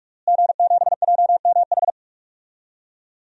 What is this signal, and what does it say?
Morse code, heard as a keyed tone, about 45 wpm, 700 Hz. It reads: G7JMH
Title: DXCC-100 Call Signs - 5 Characters - Repeated 3x - 45wpm